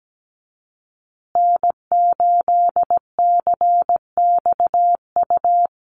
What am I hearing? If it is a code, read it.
Morse code, heard as a keyed tone, about 17 wpm, 700 Hz. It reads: N8CXU
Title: USA Call Signs Repeated 2x 17wpm